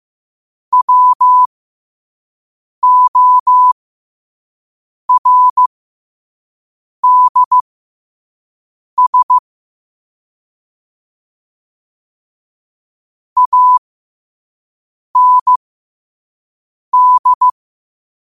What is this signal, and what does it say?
Morse code, heard as a keyed tone, about 15 wpm, spaced as for 5 wpm, 1000 Hz. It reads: WORDS AND
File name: qotdfiles / 5wpm